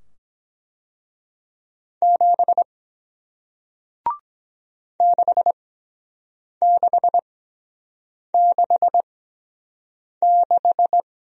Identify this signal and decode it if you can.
Morse code, heard as a keyed tone, about 23 wpm, spaced as for 7 wpm, 700 Hz. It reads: 7 6666